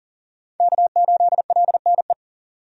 Morse code, heard as a keyed tone, about 40 wpm, 700 Hz. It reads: K8LNE
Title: USA Call Signs - 5 Characters - 40wpm